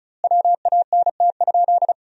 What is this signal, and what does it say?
Morse code, heard as a keyed tone, about 34 wpm, 700 Hz. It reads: WANT?
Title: 2x Word-Spacing - Sentences from Top 200 Words - 35wpm